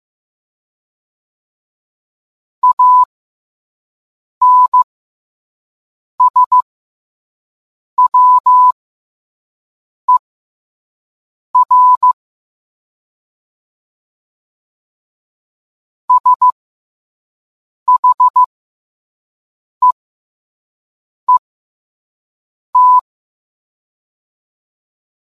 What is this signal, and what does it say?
Morse code, heard as a keyed tone, about 15 wpm, spaced as for 5 wpm, 1000 Hz. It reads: ANSWER SHEET